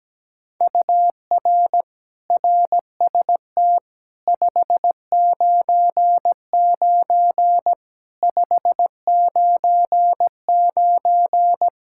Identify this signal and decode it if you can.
Morse code, heard as a keyed tone, about 17 wpm, 700 Hz. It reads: UR RST 599 599
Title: Practice QSO - Exchange 1 - Beginning - 17wpm